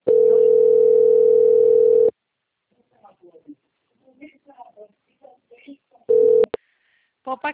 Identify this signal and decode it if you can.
Ringback tone (North America), 440 + 480 Hz, with speech.
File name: out- - - - - .22194